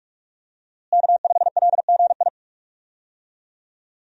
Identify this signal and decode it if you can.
Morse code, heard as a keyed tone, about 46 wpm, 700 Hz. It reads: K5LGI
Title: USA Call Signs - 5 Characters - Repeated 3x - 45wpm